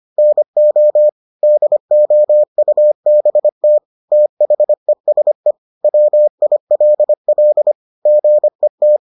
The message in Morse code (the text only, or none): NO DOUBT THESE WILL GET